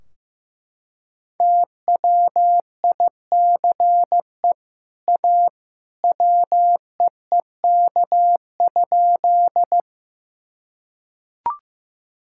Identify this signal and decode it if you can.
Morse code, heard as a keyed tone, about 15 wpm, 700 Hz. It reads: TWICE A WEEK?
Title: Sentences from Top 1600 Words 15wpm